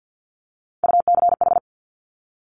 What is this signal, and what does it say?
Morse code, heard as a keyed tone, about 50 wpm, 700 Hz. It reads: UCH
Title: Top Three Letter Combinations in Words 50wpm